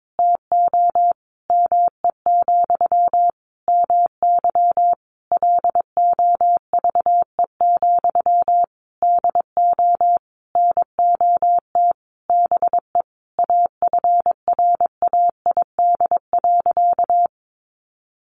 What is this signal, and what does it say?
Morse code, heard as a keyed tone, about 22 wpm, 700 Hz. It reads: TO ME, MY LOVE, DO NOT BE AFRAID.
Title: Sentences from Top 1500 Words 22wpm